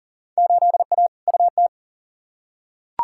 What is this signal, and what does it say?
Morse code, heard as a keyed tone, about 40 wpm, 700 Hz. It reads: 8A UT
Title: Field Day Exchanges Encore - 40wpm